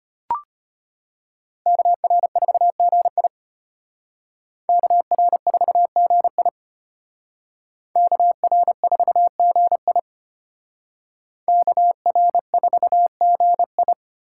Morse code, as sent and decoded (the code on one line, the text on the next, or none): -.- .-. ....- --. .. / -.- .-. ....- --. .. / -.- .-. ....- --. .. / -.- .-. ....- --. ..
KR4GI KR4GI KR4GI KR4GI